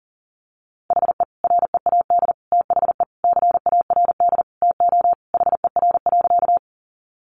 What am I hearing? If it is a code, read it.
Morse code, heard as a keyed tone, about 40 wpm, 700 Hz. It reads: HE READ THE CARD TO HER.
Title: Sentences from Top 600 Words 40wpm